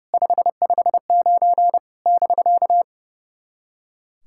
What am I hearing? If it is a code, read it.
Morse code, heard as a keyed tone, about 30 wpm, 700 Hz. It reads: 559 <BK>
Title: SOTA QSO Practice 30wpm